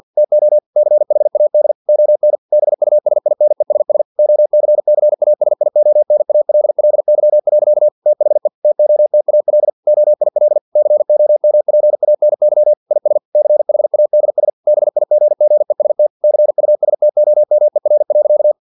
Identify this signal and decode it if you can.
Morse code, heard as a keyed tone, about 50 wpm, 600 Hz. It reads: TO CHAD ON BUSINESS OCCASIONALLY. THE TOTAL OIL COMPANY IS CHADS BIGGEST CUSTOMER.
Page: Never 49wpm